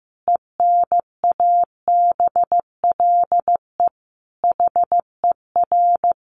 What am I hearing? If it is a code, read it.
Morse code, heard as a keyed tone, about 15 wpm, 700 Hz. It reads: ENABLE HER
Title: Sentences from Top 1500 Words 15wpm